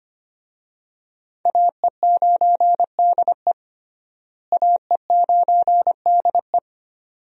Morse code, heard as a keyed tone, about 25 wpm, 700 Hz. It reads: AE9DE AE9DE
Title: USA Call Signs - 5 Characters - Repeated 2x - 25wpm